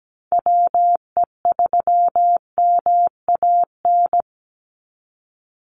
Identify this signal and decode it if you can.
Morse code, heard as a keyed tone, about 17 wpm, 700 Hz. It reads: WE3MAN
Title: USA Call Signs 17wpm